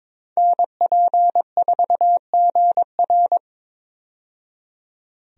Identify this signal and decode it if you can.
Morse code, heard as a keyed tone, about 22 wpm, 700 Hz. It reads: NP4GR
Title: USA Call Signs - 5 Characters - 22wpm